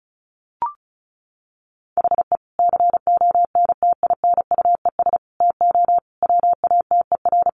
Morse code, heard as a keyed tone, about 34 wpm, 700 Hz. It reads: HE CONTINUES TO WATER
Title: Easy Phrases 35wpm